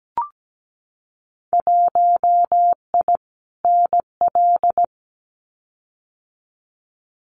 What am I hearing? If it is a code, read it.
Morse code, heard as a keyed tone, about 17 wpm, 700 Hz. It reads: 1I NL
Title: Winter Field Day Exchanges Encore 17wpm